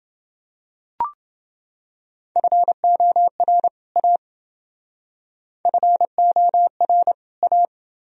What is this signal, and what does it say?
Morse code, heard as a keyed tone, about 28 wpm, 700 Hz. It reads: FOR A FOR A